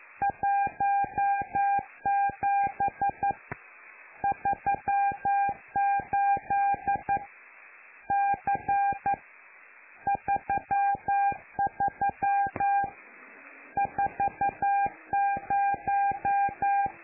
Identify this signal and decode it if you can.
Morse code, heard as a keyed tone, about 11 wpm, 800 Hz. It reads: 17 38 C 33 40
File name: lu4aao / Globo Amsat